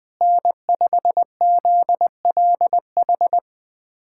Morse code, heard as a keyed tone, about 21 wpm, 700 Hz. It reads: N5ZLH
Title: USA Callsign DE Callsign - 5 Characters - 20wpm